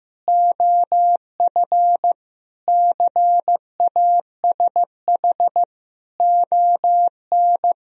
Morse code, heard as a keyed tone, about 14 wpm, 700 Hz. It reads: OF CASH ON
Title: Sentences from Top 1000 Words - Review Part 4 - 15wpm